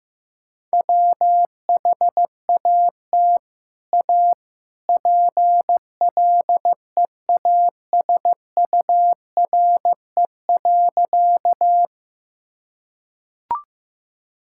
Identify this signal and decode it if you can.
Morse code, heard as a keyed tone, about 15 wpm, 700 Hz. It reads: WHAT A PLEASURE.
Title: Sentences from Top 1600 Words 15wpm